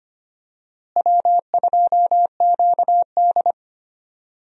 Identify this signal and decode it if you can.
Morse code, heard as a keyed tone, about 25 wpm, 700 Hz. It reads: W2QD
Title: USA Call Signs - 4 Characters - Repeated 3x - 25wpm